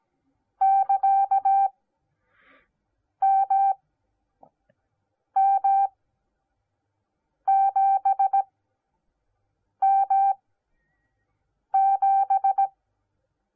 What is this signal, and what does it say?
Morse code, heard as a keyed tone, about 17 wpm, spaced as for 5 wpm, 780 Hz. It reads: <KA> MM7M7